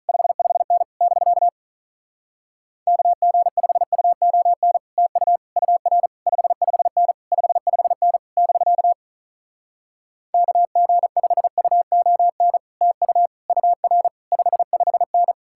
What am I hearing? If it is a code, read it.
Morse code, heard as a keyed tone, about 47 wpm, 700 Hz. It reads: 55N <BK> KG5UON TU UR 55N 55N <BK> KG5UON TU UR 55N